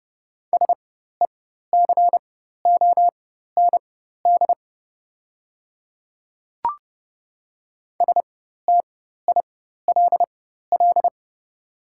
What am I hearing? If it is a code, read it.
Morse code, heard as a keyed tone, about 30 wpm, spaced as for 14 wpm, 700 Hz. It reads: SECOND STILL
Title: 4x Character Spacing - Top 300 Words - 30wpm